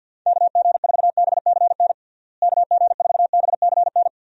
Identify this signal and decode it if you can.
Morse code, heard as a keyed tone, about 52 wpm, 700 Hz. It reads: KG4BCN KG4BCN
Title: USA Call Signs - 6 Characters - Repeated 3x - 50wpm